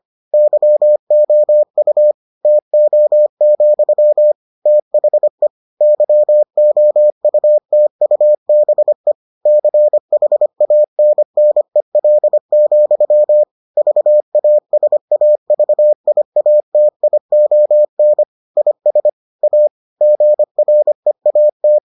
Morse code, heard as a keyed tone, about 25 wpm, 600 Hz. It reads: YOU TO, THE YOUTUBE CHANNEL, VASAVIATION IS A GREAT